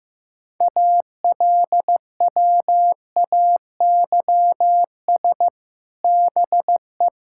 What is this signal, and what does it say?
Morse code, heard as a keyed tone, about 15 wpm, 700 Hz. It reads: ALWAYS BE
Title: Sentences from Top 1600 Words 15wpm